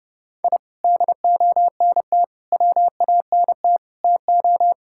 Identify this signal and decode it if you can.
Morse code, heard as a keyed tone, about 32 wpm, 700 Hz. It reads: I DONT WANT TO
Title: Sentences from Top 1000 Words - Review Part 5 - 30wpm